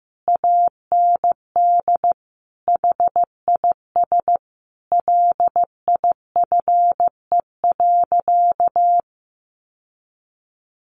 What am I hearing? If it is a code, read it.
Morse code, heard as a keyed tone, about 15 wpm, 700 Hz. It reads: AND HIS LIFE.
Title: Sentences from Top 1300 Words 15wpm